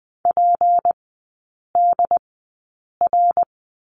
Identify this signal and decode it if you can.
Morse code, heard as a keyed tone, about 20 wpm, spaced as for 8 wpm, 700 Hz. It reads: PDR
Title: 152 Lesson 25e - Letter p - Three Characters 20wpm